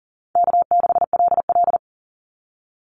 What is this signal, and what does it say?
Morse code, heard as a keyed tone, about 40 wpm, 700 Hz. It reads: K6LL
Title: Call Signs Repeated 3x - 40wpm